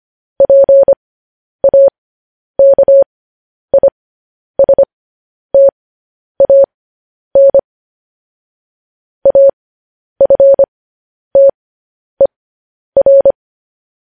Morse code, heard as a keyed tone, about 25 wpm, spaced as for 10 wpm, 550 Hz. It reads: PAKISTAN AFTER
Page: news headlines in morse code at 10 wpm